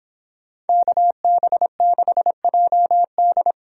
Morse code, heard as a keyed tone, about 26 wpm, 700 Hz. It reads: KB6JD